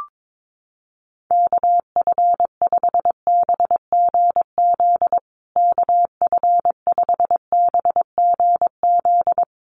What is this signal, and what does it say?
Morse code, heard as a keyed tone, about 22 wpm, 700 Hz. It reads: KF5BGZ KF5BGZ
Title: USA Call Signs Repeated 3x 22wpm